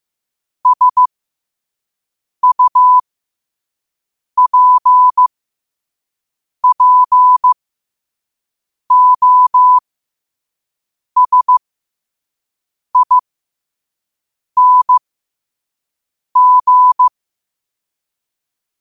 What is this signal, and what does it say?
Morse code, heard as a keyed tone, about 15 wpm, spaced as for 5 wpm, 1000 Hz. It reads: SUPPOSING
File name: qotdfiles / 5wpm